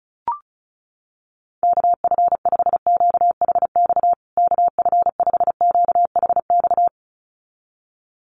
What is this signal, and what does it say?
Morse code, heard as a keyed tone, about 34 wpm, 700 Hz. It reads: KF5QHX KF5QHX
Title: USA Call Signs Repeated 2x 35wpm